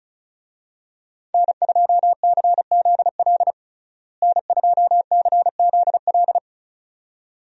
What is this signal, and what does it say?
Morse code, heard as a keyed tone, about 34 wpm, 700 Hz. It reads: N2CZL N2CZL
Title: USA Call Signs - 5 Characters - Repeated 2x - 35wpm